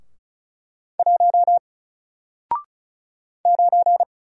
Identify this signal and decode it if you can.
Morse code, heard as a keyed tone, about 35 wpm, 700 Hz. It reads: 1 9
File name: Single-Letter-Number-35wpm